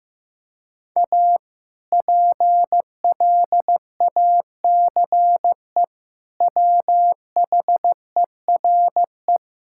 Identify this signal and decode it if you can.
Morse code, heard as a keyed tone, about 15 wpm, 700 Hz. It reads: A PLACE WHERE
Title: Sentences from Top 1000 Words - Review Part 5 - 15wpm